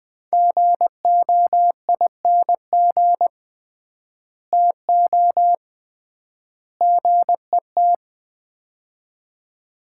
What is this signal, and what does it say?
Morse code, heard as a keyed tone, about 20 wpm, 700 Hz. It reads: GOING TO GET
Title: 3x Word-Spacing - Sets of 3 Words - 20wpm